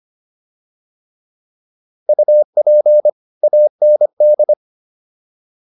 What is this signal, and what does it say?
Morse code, heard as a keyed tone, about 25 wpm, 600 Hz. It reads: UP AND